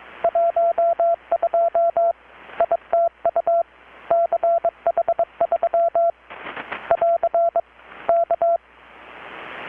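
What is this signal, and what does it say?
Morse code, heard as a keyed tone, about 22 wpm, 680 Hz. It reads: 12 ITU CH3 <AR> K
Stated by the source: Morse from KPH's Night of Nights XX maritime radiotelegraphy event, received in Chiapas, Mexico about 3,800 km away on a Kenwood R-600 with a grounded 10 m long-wire antenna.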